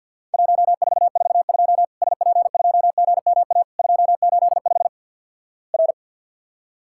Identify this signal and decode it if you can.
Morse code, heard as a keyed tone, about 53 wpm, 700 Hz. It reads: 1442 SP2GMA 285 R
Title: Worked All Europe DX Contest - QTC Exchange with Repeats - 50wpm